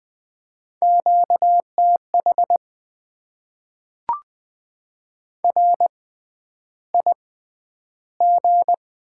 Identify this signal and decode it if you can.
Morse code, heard as a keyed tone, about 20 wpm, spaced as for 10 wpm, 700 Hz. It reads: QTH RIG
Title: Top 20 QSO Elements - 6x Character Spacing - 20wpm